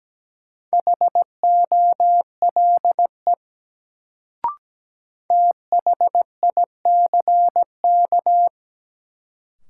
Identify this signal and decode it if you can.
Morse code, heard as a keyed tone, about 17 wpm, 700 Hz. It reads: HOLE THICK
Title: Sentences from Top 1600 Words - New Word Drill Alt - 17wpm